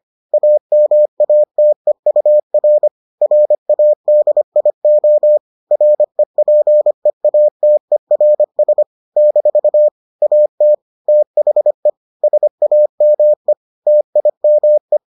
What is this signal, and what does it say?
Morse code, heard as a keyed tone, about 25 wpm, 600 Hz. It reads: AMATEUR RADIO REPEATERS - AT THE SAME TIME